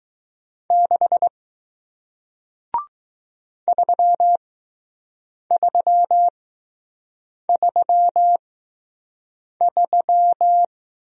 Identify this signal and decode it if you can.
Morse code, heard as a keyed tone, about 20 wpm, spaced as for 6 wpm, 700 Hz. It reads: 6 3333